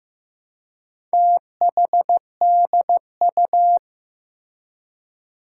Morse code, heard as a keyed tone, about 15 wpm, 700 Hz. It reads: THDU
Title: Four Challenging Letter-Number Combinations 15wpm